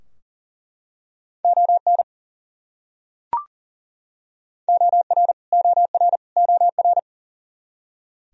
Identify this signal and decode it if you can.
Morse code, heard as a keyed tone, about 40 wpm, 700 Hz. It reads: ON OR OR OR
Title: Top 100 Words Repeated 40wpm